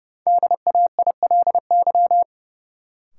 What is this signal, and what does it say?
Morse code, heard as a keyed tone, about 30 wpm, 700 Hz. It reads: DAILY
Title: Sentences from Top 1100 Words - New Word Drill - 30wpm